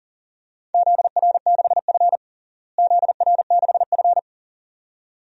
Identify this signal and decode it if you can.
Morse code, heard as a keyed tone, about 40 wpm, 700 Hz. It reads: ZR6F ZR6F
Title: DXCC-100 Call Signs - 4 Characters - Repeated 2x - 40wpm